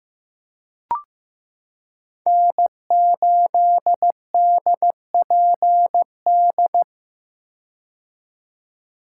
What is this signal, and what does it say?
Morse code, heard as a keyed tone, about 15 wpm, 700 Hz. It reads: N8DPD
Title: K1USN SST QSO Practice - 15wpm